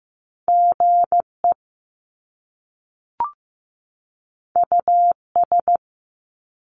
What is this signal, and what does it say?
Morse code, heard as a keyed tone, about 15 wpm, 700 Hz. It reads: GE US
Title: Top Two Letter Combinations in Words 15wpm